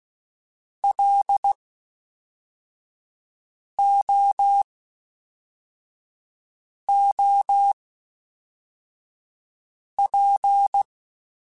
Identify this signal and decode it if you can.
Morse code, heard as a keyed tone, about 16 wpm, spaced as for 4 wpm, 800 Hz. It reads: LOOP
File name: (spaced) / FromSMs spaced mp3s / Antennas sp@5